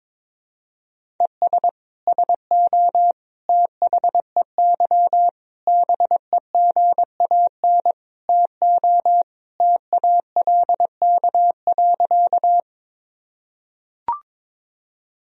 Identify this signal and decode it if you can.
Morse code, heard as a keyed tone, about 22 wpm, 700 Hz. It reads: ES SO THEY BEGAN TO TALK.
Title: Sentences from Top 400 Words and Abbreviations 22wpm